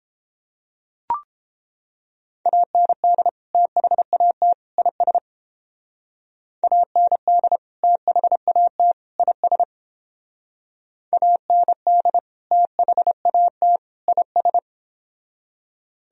Morse code, heard as a keyed tone, about 29 wpm, 700 Hz. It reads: AND THAT IS AND THAT IS AND THAT IS